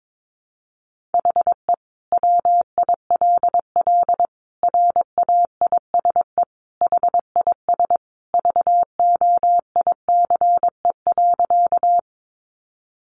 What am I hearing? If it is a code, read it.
Morse code, heard as a keyed tone, about 22 wpm, 700 Hz. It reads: HE WILL RAISE HIS VOICE.
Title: Sentences from Top 1400 Words 22wpm